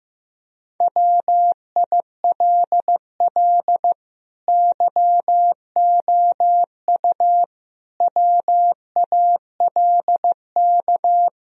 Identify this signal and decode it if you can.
Morse code, heard as a keyed tone, about 16 wpm, 700 Hz. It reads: WILL YOU WALK
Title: Sentences from Top 1000 Words - Review Part 1 - 15wpm